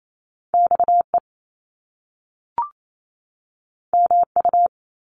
Morse code, heard as a keyed tone, about 28 wpm, 700 Hz. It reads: XE MU
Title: Top Two Letter Combinations in Words 28wpm